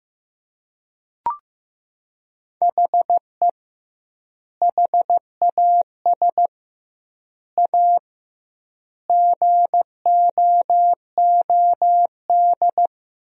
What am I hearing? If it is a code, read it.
Morse code, heard as a keyed tone, about 15 wpm, 700 Hz. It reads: HE HAS A GOOD
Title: 2x Word-Spacing - Sentences from Top 500 Words - 15wpm